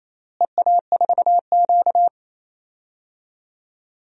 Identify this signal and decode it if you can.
Morse code, heard as a keyed tone, about 28 wpm, 700 Hz. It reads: EA4Q
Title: DXCC-100 Call Signs - 4 Characters - Repeated 2x - 28wpm